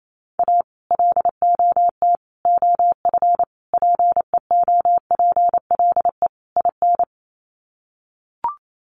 Morse code, heard as a keyed tone, about 28 wpm, 700 Hz. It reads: A LOT OF PEOPLE IN
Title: Sets of 5 Words 28wpm